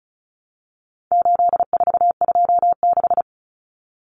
Morse code, heard as a keyed tone, about 34 wpm, 700 Hz. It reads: 8426
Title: Numbers - 4-Digit 35wpm